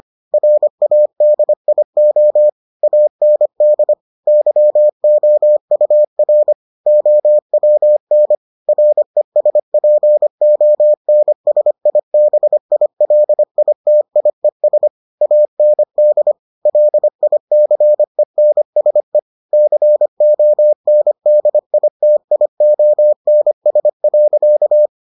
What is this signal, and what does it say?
Morse code, heard as a keyed tone, about 25 wpm, 600 Hz. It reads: RADIO AND YOUR OWN RESPONSIBILITIES AND LICENSE CONDITIONS.